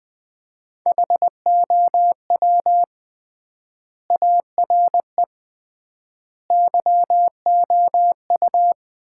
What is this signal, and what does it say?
Morse code, heard as a keyed tone, about 20 wpm, 700 Hz. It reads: HOW ARE YOU
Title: 3x Word-Spacing - Sets of 5 Words - 20wpm